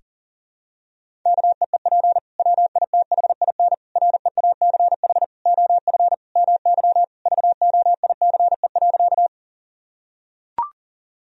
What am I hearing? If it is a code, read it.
Morse code, heard as a keyed tone, about 40 wpm, 690 Hz. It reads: KEEP WITHIN REACH OF MY VOICE.
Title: Sentences from Top 1000 Words - Review Part 4 - 40wpm